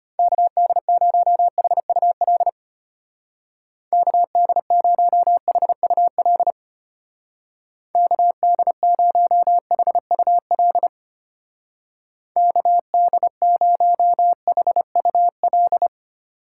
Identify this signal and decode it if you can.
Morse code, heard as a keyed tone, about 32 wpm, 700 Hz. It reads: KD0HUL KD0HUL KD0HUL KD0HUL